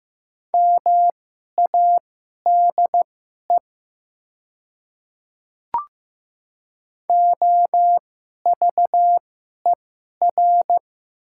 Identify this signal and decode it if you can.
Morse code, heard as a keyed tone, about 15 wpm, spaced as for 11 wpm, 700 Hz. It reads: MADE OVER
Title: 2x Character Spacing - Top 200 Words - 15wpm